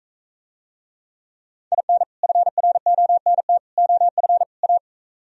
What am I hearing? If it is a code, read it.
Morse code, heard as a keyed tone, about 44 wpm, 700 Hz. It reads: IN FRONT OF A